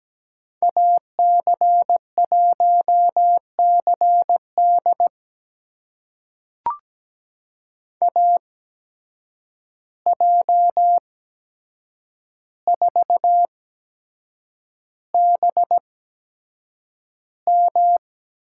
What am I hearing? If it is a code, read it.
Morse code, heard as a keyed tone, about 17 wpm, spaced as for 7 wpm, 700 Hz. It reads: AC1CD AJ4BM
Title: USA Call Signs - 8x Character Spacing - 17wpm